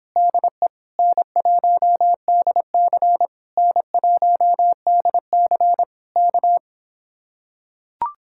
Morse code, heard as a keyed tone, about 26 wpm, 700 Hz. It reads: DE N1DC N1DC K